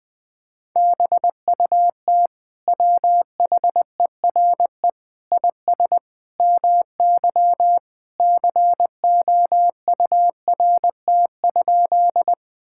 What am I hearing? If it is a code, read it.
Morse code, heard as a keyed tone, about 20 wpm, 700 Hz. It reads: BUT WHERE IS MY COURT?
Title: Sentences from Top 1600 Words 20wpm